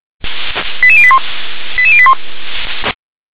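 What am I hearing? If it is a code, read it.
Five-tone selective call (ZVEI), heard as quick runs of five tones, tones of about 70 ms. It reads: 88081 88081
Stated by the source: ZVEI five-tone selective call 88783 received on an SDR.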